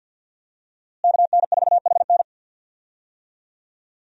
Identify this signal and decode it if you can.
Morse code, heard as a keyed tone, about 48 wpm, 700 Hz. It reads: KN4HN
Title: USA Call Signs - 5 Characters - Repeated 3x - 50wpm